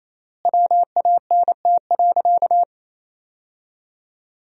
Morse code, heard as a keyed tone, about 28 wpm, 700 Hz. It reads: WANT.
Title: 3x Word-Spacing - Sentences from Top 200 Words - 28wpm